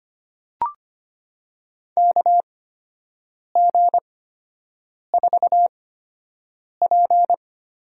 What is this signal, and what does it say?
Morse code, heard as a keyed tone, about 25 wpm, spaced as for 7 wpm, 700 Hz. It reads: KG4P
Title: USA Call Signs - 6 Characters - 8x Character Spacing - 25wpm